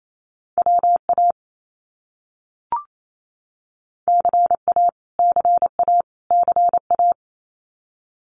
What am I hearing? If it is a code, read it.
Morse code, heard as a keyed tone, about 28 wpm, 700 Hz. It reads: WA CA CA CA
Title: US States Repeated 3x 28wpm